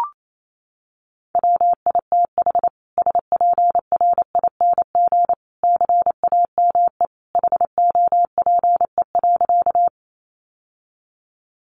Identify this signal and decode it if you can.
Morse code, heard as a keyed tone, about 28 wpm, 700 Hz. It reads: WITH SPRING CAME HOPE.
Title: Sentences from Top 1200 Words 28wpm